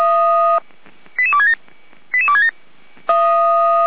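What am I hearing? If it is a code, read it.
Five-tone selective call (ZVEI), heard as quick runs of five tones, tones of about 70 ms. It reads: 88267 88267
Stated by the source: Five-tone ZVEI selective call 88211 received on SDR.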